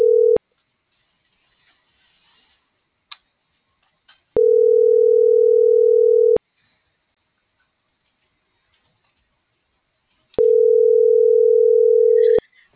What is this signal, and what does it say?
Ringback tone (North America), 440 + 480 Hz.